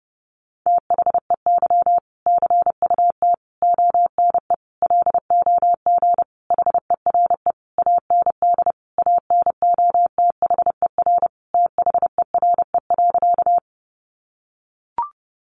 Morse code, heard as a keyed tone, about 30 wpm, 700 Hz. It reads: THEY CUT ONE LOG HERE AND ANOTHER THERE.
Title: Sentences from Top 1400 Words 30wpm